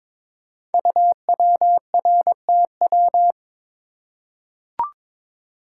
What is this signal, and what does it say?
Morse code, heard as a keyed tone, about 22 wpm, 700 Hz. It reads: UWRTW
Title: Code Groups - 5 Letters - 22wpm